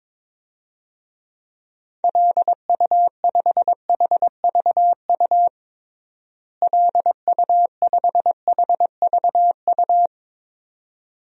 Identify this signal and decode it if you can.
Morse code, heard as a keyed tone, about 22 wpm, 700 Hz. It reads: LU5HVU LU5HVU
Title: DXCC-100 Call Signs - 6 Characters - Repeated 2x - 22wpm